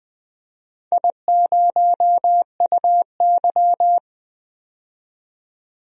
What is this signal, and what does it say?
Morse code, heard as a keyed tone, about 20 wpm, 700 Hz. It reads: I0UY
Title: DXCC-100 Call Signs - 4 Characters - 20wpm